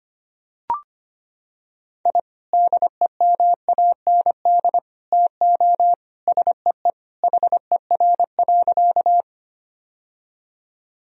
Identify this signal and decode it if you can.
Morse code, heard as a keyed tone, about 25 wpm, 700 Hz. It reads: I DEMAND TO SEE HER.
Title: Sentences from Top 1000 Words - Review Part 4 - 25wpm